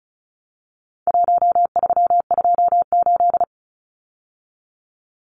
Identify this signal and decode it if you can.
Morse code, heard as a keyed tone, about 34 wpm, 700 Hz. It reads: 1328
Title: Numbers - Contest Style - 4-Digit 35wpm